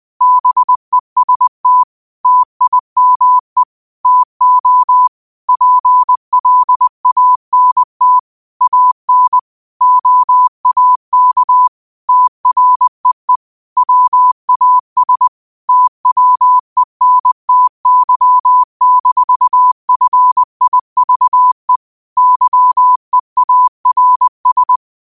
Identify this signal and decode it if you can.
Morse code, heard as a keyed tone, about 20 wpm, 1000 Hz. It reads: BEST TIME TO PLANT AN OAK TREE WAS TWENTY-FIVE YEARS
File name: qotdfiles / 20wpm